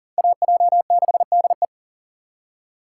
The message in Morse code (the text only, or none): AJ6DE